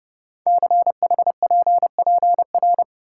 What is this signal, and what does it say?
Morse code, heard as a keyed tone, about 28 wpm, 700 Hz. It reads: CHPPR
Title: Code Groups - 5 Letters - 30wpm